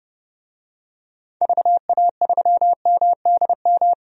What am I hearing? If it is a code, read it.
Morse code, heard as a keyed tone, about 30 wpm, 700 Hz. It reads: VA3MDM
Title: DXCC-100 Call Signs - 6 Characters - 30wpm